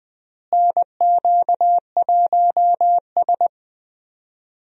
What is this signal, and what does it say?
Morse code, heard as a keyed tone, about 21 wpm, 700 Hz. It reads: NQ1S
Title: USA Call Signs - 4 Characters - Repeated 2x - 20wpm